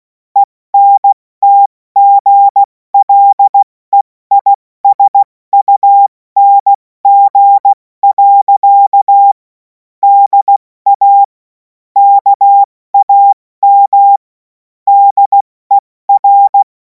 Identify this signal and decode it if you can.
Morse code, heard as a keyed tone, about 16 wpm, 800 Hz. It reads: ENTGLEISUNG. DA KAM DER
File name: Morsekurs / Witze / Witz 03. Februar 2026 70 F80